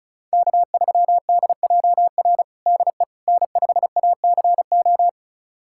Morse code, heard as a keyed tone, about 34 wpm, 700 Hz. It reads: K3DJR DE N5ACO
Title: USA Callsign DE Callsign - 5 Characters - 35wpm